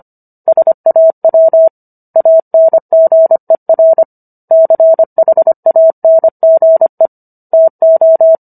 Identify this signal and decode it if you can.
Morse code, heard as a keyed tone, about 25 wpm, 650 Hz. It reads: SAW ANGER CHANGE TO